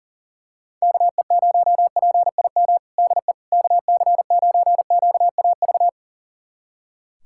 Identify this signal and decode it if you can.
Morse code, heard as a keyed tone, about 40 wpm, 700 Hz. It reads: KE0PIM DE KC9QAV
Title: USA Callsign DE Callsign - 6 Characters - 40wpm